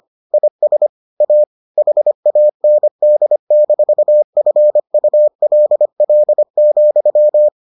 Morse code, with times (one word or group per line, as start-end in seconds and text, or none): 0.3-0.9 IS
1.2-1.4 A
1.8-7.5 HAND-FULL,